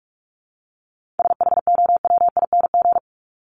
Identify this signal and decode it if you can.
Morse code, heard as a keyed tone, about 44 wpm, 700 Hz. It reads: SHOWING
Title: ING Suffix - Top 100 Words with Base Words 45wpm